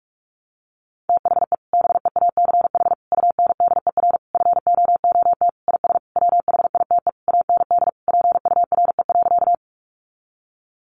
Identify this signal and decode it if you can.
Morse code, heard as a keyed tone, about 44 wpm, 700 Hz. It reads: THE BEACH UNDER FOOT IS WHITE AND PURE.
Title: Sentences from Top 1500 Words 45wpm